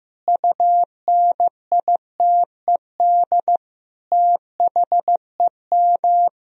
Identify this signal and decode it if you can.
Morse code, heard as a keyed tone, about 15 wpm, 700 Hz. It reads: UNITED THEM
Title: Sentences from Top 1000 Words - Review Part 4 - 15wpm